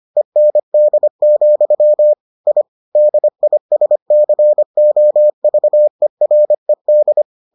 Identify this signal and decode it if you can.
Morse code, heard as a keyed tone, about 25 wpm, 600 Hz. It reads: END, I DISCOVERED